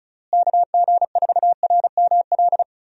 Morse code, heard as a keyed tone, about 36 wpm, 700 Hz. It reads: KG4RML
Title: USA Call Signs - 6 Characters - Repeated 2x - 35wpm